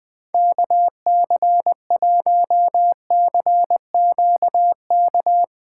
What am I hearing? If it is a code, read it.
Morse code, heard as a keyed tone, about 20 wpm, 700 Hz. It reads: KC1CQK
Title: USA Callsign DE Callsign 20wpm